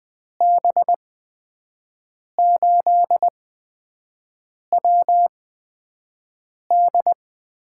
Morse code, heard as a keyed tone, about 20 wpm, spaced as for 5 wpm, 700 Hz. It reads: B8WD
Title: USA Call Signs - 8x Character Spacing - 20wpm